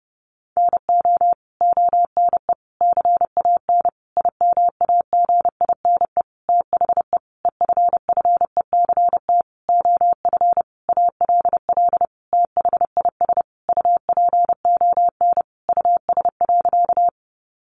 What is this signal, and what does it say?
Morse code, heard as a keyed tone, about 30 wpm, 700 Hz. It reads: NO ONE CAN IMAGINE THE EFFECT OF ALL THIS UPON US.
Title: Sentences from Top 1400 Words 30wpm